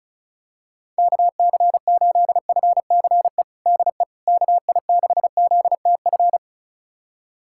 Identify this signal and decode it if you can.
Morse code, heard as a keyed tone, about 34 wpm, 700 Hz. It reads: KC8FCE DE KI6ZTF
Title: USA Callsign DE Callsign - 6 Characters - 35wpm